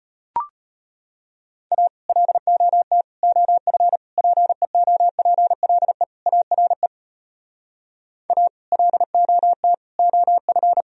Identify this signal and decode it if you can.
Morse code, heard as a keyed tone, about 33 wpm, 700 Hz. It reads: A LOT OF PEOPLE ARE A LOT OF